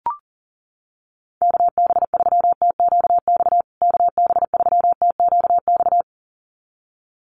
Morse code, heard as a keyed tone, about 40 wpm, 700 Hz. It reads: KB3TQX KB3TQX
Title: USA Call Signs Repeated 2x 40wpm